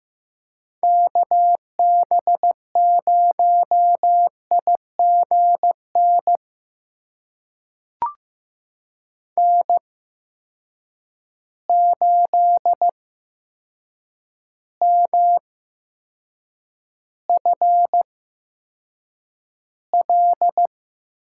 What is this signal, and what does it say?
Morse code, heard as a keyed tone, about 15 wpm, spaced as for 7 wpm, 700 Hz. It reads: KB0IGN N8MFL
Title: USA Call Signs - 8x Character Spacing - 15wpm